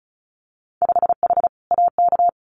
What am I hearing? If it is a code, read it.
Morse code, heard as a keyed tone, about 34 wpm, 700 Hz. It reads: 5H AK
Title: Winter Field Day Exchanges Encore 35wpm